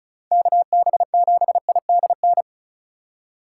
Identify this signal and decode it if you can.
Morse code, heard as a keyed tone, about 34 wpm, 700 Hz. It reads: KB7IDN
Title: USA Call Signs - 6 Characters - 8x Character Spacing - 35wpm